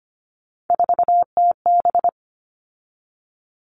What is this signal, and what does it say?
Morse code, heard as a keyed tone, about 25 wpm, 700 Hz. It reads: 4TB
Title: All Cut Numbers - 3-Digit 25wpm